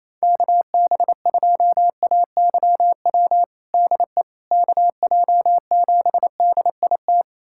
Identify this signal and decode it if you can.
Morse code, heard as a keyed tone, about 28 wpm, 700 Hz. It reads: KB2AYW DE KJ7DIT
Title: USA Callsign DE Callsign - 6 Characters - 28wpm